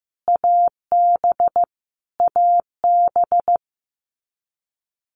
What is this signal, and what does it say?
Morse code, heard as a keyed tone, about 15 wpm, 700 Hz. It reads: AB AB
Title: Canadian Provinces and Territories - Repeated 3x - 15wpm